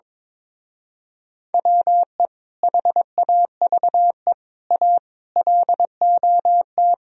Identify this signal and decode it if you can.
Morse code, heard as a keyed tone, about 23 wpm, 700 Hz. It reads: WE HAVE A LOT
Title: Speed-Racing - Sets of 5 Words 22wpm